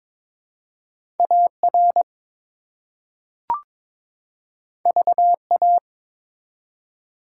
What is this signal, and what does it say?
Morse code, heard as a keyed tone, about 23 wpm, 700 Hz. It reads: AR VA
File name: ARRL Sections v2 - 22wpm